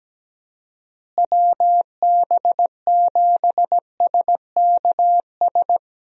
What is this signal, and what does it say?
Morse code, heard as a keyed tone, about 17 wpm, 700 Hz. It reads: WB7SKS
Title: USA Call Signs - 6 Characters - Repeated 3x - 17wpm